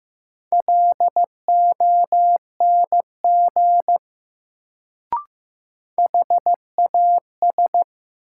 Morse code, heard as a keyed tone, about 15 wpm, 700 Hz. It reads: LONG HAS
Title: Sentences from Top 100 Words - New Word Drill - 15wpm